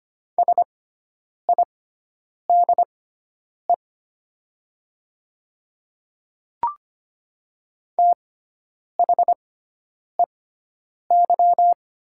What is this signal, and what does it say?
Morse code, heard as a keyed tone, about 25 wpm, spaced as for 9 wpm, 700 Hz. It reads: SIDE THEY
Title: 6x Character Spacing - Top 100 Words - 25wpm